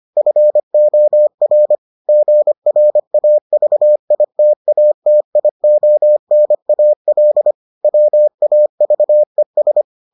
Morse code, heard as a keyed tone, about 25 wpm, 600 Hz. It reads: FOR GRAVITATIONAL WAVES